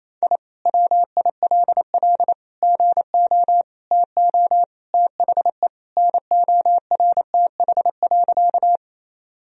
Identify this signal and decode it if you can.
Morse code, heard as a keyed tone, about 28 wpm, 700 Hz. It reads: I WILL GO TO THE NORTH.
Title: Sentences from Top 1000 Words - Review Part 1 - 28wpm